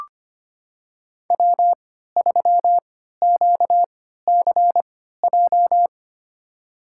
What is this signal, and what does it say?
Morse code, heard as a keyed tone, about 25 wpm, spaced as for 14 wpm, 700 Hz. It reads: W3QCJ
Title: USA Call Signs - 5 Characters - 3x Character Spacing - 25wpm